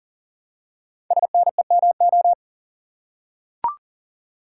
Morse code, heard as a keyed tone, about 40 wpm, 700 Hz. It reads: SNEMO
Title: Code Groups - 5 Letters - 40wpm